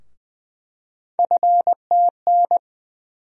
Morse code, heard as a keyed tone, about 20 wpm, 700 Hz. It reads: FTN
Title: USA Call Sign Postfixes - 3 Letters 20wpm